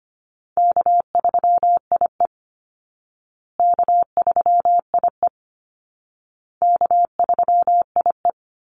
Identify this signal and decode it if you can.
Morse code, heard as a keyed tone, about 25 wpm, 700 Hz. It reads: K3IE K3IE K3IE
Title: Easy Call Signs Repeated 3x - 25wpm